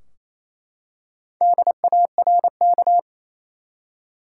Morse code, heard as a keyed tone, about 28 wpm, 700 Hz. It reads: DARK
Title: Four Letter Words - 28wpm